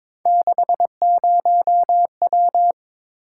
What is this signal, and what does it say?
Morse code, heard as a keyed tone, about 26 wpm, 700 Hz. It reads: 60W
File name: Speed-Racing - Three Letter-Number Combinations - 22wpm